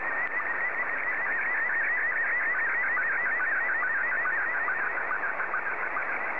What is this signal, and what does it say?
SSTV (slow-scan television), Scottie 1.